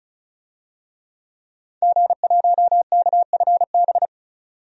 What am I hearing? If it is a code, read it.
Morse code, heard as a keyed tone, about 36 wpm, 700 Hz. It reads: G1KFB
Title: DXCC-100 Call Signs - 5 Characters - 35wpm